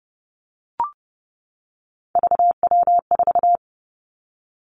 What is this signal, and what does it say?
Morse code, heard as a keyed tone, about 30 wpm, 700 Hz. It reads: VW4
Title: Three Challenging Letter-Number Combinations 30wpm